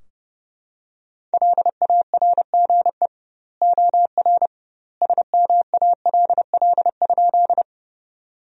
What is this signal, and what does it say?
Morse code, heard as a keyed tone, about 30 wpm, 700 Hz. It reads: LARGE OR SMALL?
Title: 2x Word-Spacing - Sentences from Top 200 Words - 30wpm